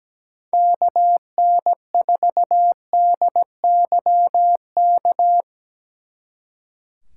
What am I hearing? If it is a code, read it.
Morse code, heard as a keyed tone, about 19 wpm, 700 Hz. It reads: KN4DYK
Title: Speed Racing - USA Call Signs 17wpm